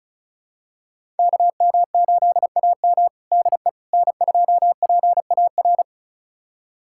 Morse code, heard as a keyed tone, about 34 wpm, 700 Hz. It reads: KM8AM DE N2PAR
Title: USA Callsign DE Callsign - 5 Characters - 35wpm